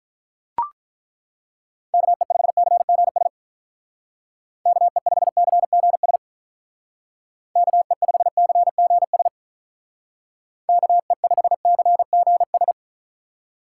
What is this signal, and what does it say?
Morse code, heard as a keyed tone, about 43 wpm, 700 Hz. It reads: KE5CGS KE5CGS KE5CGS KE5CGS
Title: Speed Racing - USA 2x3 Call Signs 35wpm